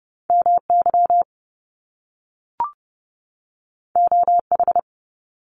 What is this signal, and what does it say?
Morse code, heard as a keyed tone, about 30 wpm, 700 Hz. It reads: MY OH
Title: Top Two Letter Combinations in Words 30wpm